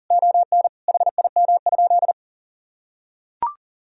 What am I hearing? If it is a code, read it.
Morse code, heard as a keyed tone, about 40 wpm, 700 Hz. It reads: ON HIM?
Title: Sentences from Top 1000 Words - Review Part 2 - 40wpm